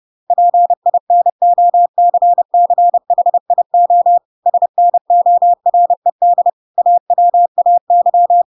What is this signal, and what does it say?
Morse code, heard as a keyed tone, about 30 wpm, 700 Hz. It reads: PINOCCHIO SNORED AWAY
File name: Pinocchio / Pinocchio ch6 30 30 ik1hge